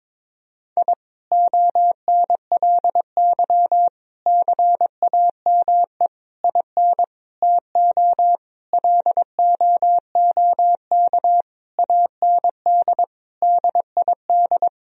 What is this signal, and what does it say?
Morse code, heard as a keyed tone, about 22 wpm, 700 Hz. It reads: I ONLY CAME IN TO LOOK AND DID
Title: Sentences from Top 1000 Words - Review Part 2 - 22wpm